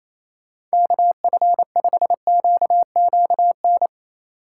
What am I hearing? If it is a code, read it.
Morse code, heard as a keyed tone, about 28 wpm, 700 Hz. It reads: KF5QQN
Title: USA Call Signs - 6 Characters - 3x Character Spacing - 28wpm